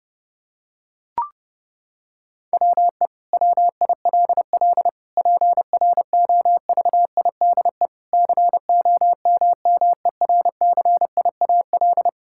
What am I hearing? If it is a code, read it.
Morse code, heard as a keyed tone, about 30 wpm, 700 Hz. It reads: WE WILL PROVIDE COMMERCIAL
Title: Sentences from Top 1000 Words - Review Part 5 - 30wpm